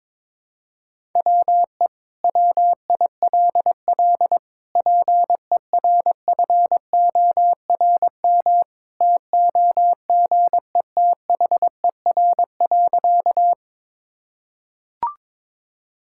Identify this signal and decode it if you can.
Morse code, heard as a keyed tone, about 21 wpm, 700 Hz. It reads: WE WILL PERFORM TOGETHER.
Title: Sentences from Top 1000 Words - Review Part 5 - 22wpm